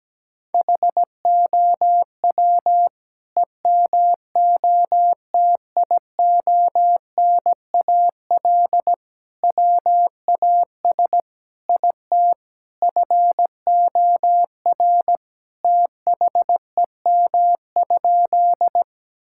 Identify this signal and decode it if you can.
Morse code, heard as a keyed tone, about 17 wpm, 700 Hz. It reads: HOW EMOTIONAL WAS IT FOR THEM?
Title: Sentences from Top 1000 Words - Review Part 5 - 17wpm